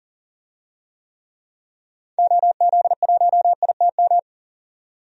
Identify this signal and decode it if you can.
Morse code, heard as a keyed tone, about 40 wpm, 700 Hz. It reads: OZ1ITM
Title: DXCC-100 Call Signs - 6 Characters - 40wpm